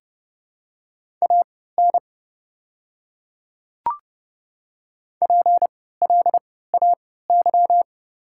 Morse code, heard as a keyed tone, about 30 wpm, spaced as for 17 wpm, 700 Hz. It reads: AN PLAY
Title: 3x Character Spacing - Top 500 Words - 30wpm